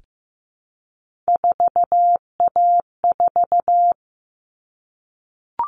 Morse code, heard as a keyed tone, about 15 wpm, 700 Hz. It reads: 4A4
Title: Common Cut Numbers - 3-Digit 15wpm